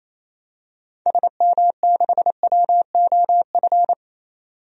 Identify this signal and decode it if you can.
Morse code, heard as a keyed tone, about 28 wpm, 700 Hz. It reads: SM6WOF
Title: DXCC-100 Call Signs - 6 Characters - 28wpm